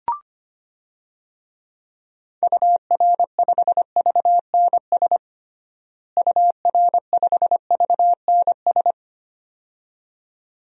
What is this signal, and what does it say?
Morse code, heard as a keyed tone, about 25 wpm, 700 Hz. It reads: UR5VNS UR5VNS
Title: DXCC-100 Call Signs - 6 Characters - Repeated 2x - 25wpm